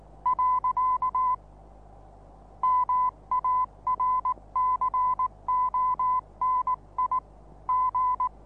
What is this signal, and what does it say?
Morse code, heard as a keyed tone, about 19 wpm, 1010 Hz. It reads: . MARCONI G
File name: repos / Morsekurs / DJ4UF Morsen26